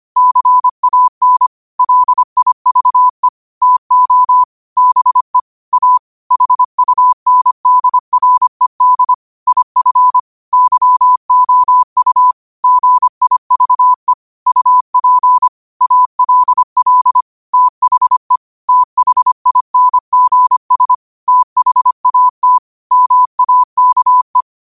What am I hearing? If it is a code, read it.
Morse code, heard as a keyed tone, about 25 wpm, 1000 Hz. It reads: CAN LIVE TO BE A HUNDRED IF YOU GIVE UP ALL THE THINGS THAT MAKE
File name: qotdfiles / 25wpm